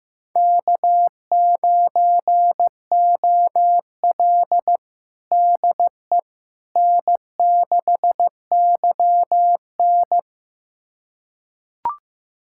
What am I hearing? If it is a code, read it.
Morse code, heard as a keyed tone, about 15 wpm, 700 Hz. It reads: K9OL DE N6YN
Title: USA Callsign DE Callsign - 4 Characters - 15wpm